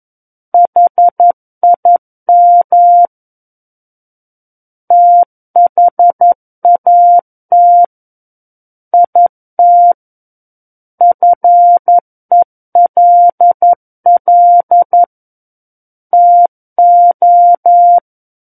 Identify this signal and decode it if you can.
Morse code, heard as a keyed tone, about 11 wpm, 700 Hz. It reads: HIM THAT IT FELL TO
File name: Book One / 02-Chapter 2 - 11 WPM